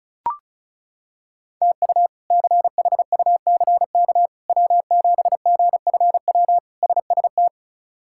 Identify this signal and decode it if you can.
Morse code, heard as a keyed tone, about 34 wpm, 700 Hz. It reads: TU CHUCK W7GFW SST
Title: K1USN SST QSO Practice - 35wpm